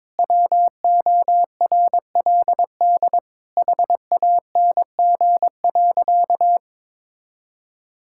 Morse code, heard as a keyed tone, about 22 wpm, 700 Hz. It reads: WORLD HANG.
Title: Sentences from Top 1600 Words 22wpm